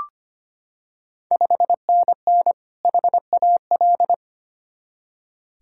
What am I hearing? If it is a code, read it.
Morse code, heard as a keyed tone, about 25 wpm, 700 Hz. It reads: 5NN HAL
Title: Ontario QSO Party Exchange 25wpm